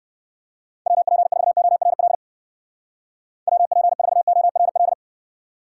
Morse code, heard as a keyed tone, about 55 wpm, 700 Hz. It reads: WP4ORL WP4ORL
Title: Speed Racing - USA 2x3 Call Signs 45wpm